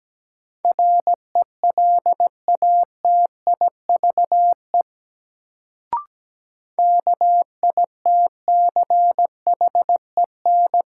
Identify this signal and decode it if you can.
Morse code, heard as a keyed tone, about 17 wpm, 700 Hz. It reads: RELATIVE KITCHEN
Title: Sentences from Top 1400 Words - New Word Drill - 17wpm